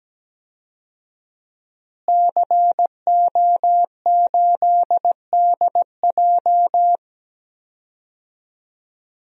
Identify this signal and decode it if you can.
Morse code, heard as a keyed tone, about 17 wpm, 700 Hz. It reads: CO8DJ
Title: DXCC-100 Call Signs - 5 Characters - Repeated 3x - 17wpm